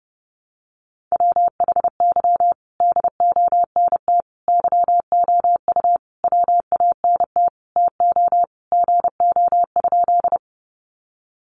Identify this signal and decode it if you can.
Morse code, heard as a keyed tone, about 30 wpm, 700 Hz. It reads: WHY DONT YOU WANT TO GO?
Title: Sentences from Top 300 Words 30wpm